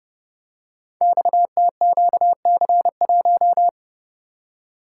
Morse code, heard as a keyed tone, about 31 wpm, 700 Hz. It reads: XTQC1
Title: Code Groups - 5 Letter:Numbers - 30wpm